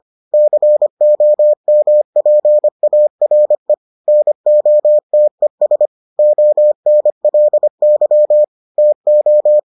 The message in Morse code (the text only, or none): COMPARE NOTES ONLY TO